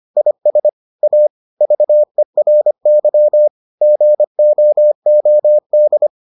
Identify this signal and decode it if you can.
Morse code, heard as a keyed tone, about 25 wpm, 600 Hz. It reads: IS A VERY GOOD